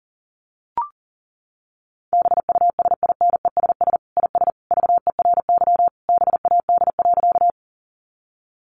Morse code, heard as a keyed tone, about 37 wpm, 700 Hz. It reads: BUSINESS IS VERY BAD.
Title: Sentences from Top 700 Words 40wpm